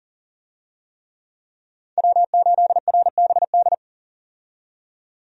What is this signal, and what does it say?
Morse code, heard as a keyed tone, about 40 wpm, 700 Hz. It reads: W8RBD
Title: USA Call Signs - 5 Characters - 40wpm